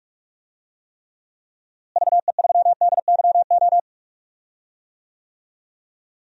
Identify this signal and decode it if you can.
Morse code, heard as a keyed tone, about 44 wpm, 700 Hz. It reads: VE3DYO
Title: DXCC-100 Call Signs - 6 Characters - Repeated 3x - 45wpm